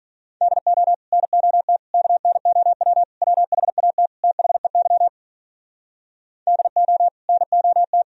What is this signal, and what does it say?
Morse code, heard as a keyed tone, about 47 wpm, 700 Hz. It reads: DO NOT KNOW WHAT THEY DO NOT